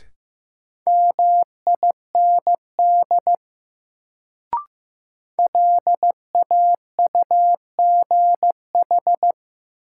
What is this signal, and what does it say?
Morse code, heard as a keyed tone, about 15 wpm, 700 Hz. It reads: MIND LAUGH
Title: Sentences from Top 500 Words - New Word Drill - 15wpm